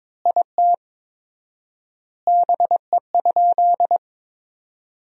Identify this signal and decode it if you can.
Morse code, heard as a keyed tone, about 22 wpm, 700 Hz. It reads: IT BE?
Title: 4x Word-Spacing - Sentences from Top 100 Words - 22wpm